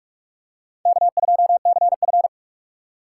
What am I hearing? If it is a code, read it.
Morse code, heard as a keyed tone, about 46 wpm, 690 Hz. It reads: K2CF
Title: USA Call Signs - 4 Characters - 2x Character Spacing - 45wpm